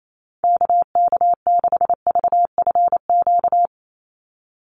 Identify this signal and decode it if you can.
Morse code, heard as a keyed tone, about 28 wpm, 700 Hz. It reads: KK6VFQ
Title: USA Call Signs 28wpm